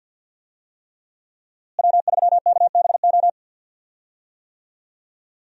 Morse code, heard as a keyed tone, about 51 wpm, 700 Hz. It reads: W3KBO
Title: USA Call Signs - 5 Characters - Repeated 3x - 50wpm